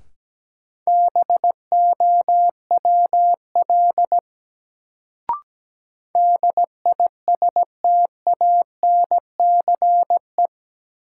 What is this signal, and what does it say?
Morse code, heard as a keyed tone, about 17 wpm, 700 Hz. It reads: BOWL DISTANCE
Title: Sentences from Top 1200 Words - New Word Drill - 17wpm